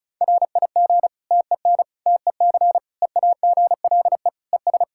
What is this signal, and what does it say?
Morse code, heard as a keyed tone, about 36 wpm, 700 Hz. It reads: RIG TEN TEC EAGLE ES
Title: Practice QSO - Exchange 2 - Info - 35wpm